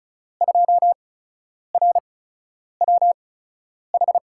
Morse code, heard as a keyed tone, about 34 wpm, spaced as for 10 wpm, 700 Hz. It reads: 2RWH
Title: USA Call Signs - 5 Characters - 8x Character Spacing - 35wpm